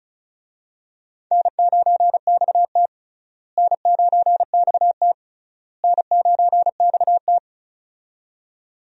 Morse code, heard as a keyed tone, about 36 wpm, 700 Hz. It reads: N9XT N9XT N9XT
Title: USA Call Signs - 4 Characters - Repeated 3x - 35wpm